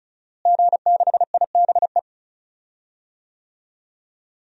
Morse code, heard as a keyed tone, about 36 wpm, 700 Hz. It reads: G6IBE
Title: DXCC-100 Call Signs - 5 Characters - Repeated 2x - 35wpm